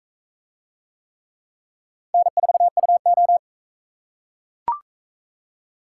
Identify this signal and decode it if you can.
Morse code, heard as a keyed tone, about 41 wpm, 700 Hz. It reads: N4UO